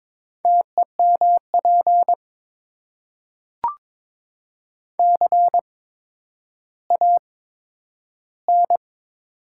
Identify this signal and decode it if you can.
Morse code, heard as a keyed tone, about 21 wpm, spaced as for 11 wpm, 700 Hz. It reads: TEMP CAN
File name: Top 90 QSO Elements - 8x Character Spacing - 22wpm